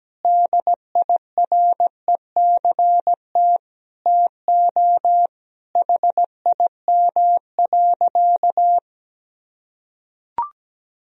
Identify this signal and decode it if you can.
Morse code, heard as a keyed tone, about 17 wpm, 700 Hz. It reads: DIRECT TO HIM.
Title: Sentences from Top 1000 Words - Review Part 2 - 17wpm